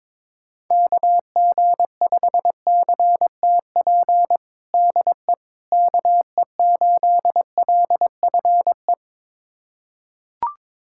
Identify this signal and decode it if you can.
Morse code, heard as a keyed tone, about 22 wpm, 700 Hz. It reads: KG5CTP DE KE8LFE
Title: USA Callsign DE Callsign - 6 Characters - 22wpm